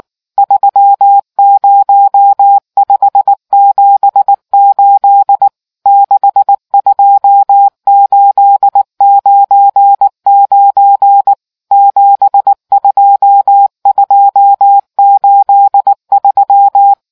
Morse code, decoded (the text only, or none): 30578 62899 72283